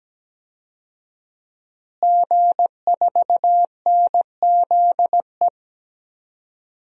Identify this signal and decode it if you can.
Morse code, heard as a keyed tone, about 17 wpm, 700 Hz. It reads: G4NZE
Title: DXCC-100 Call Signs - 5 Characters - Repeated 2x - 17wpm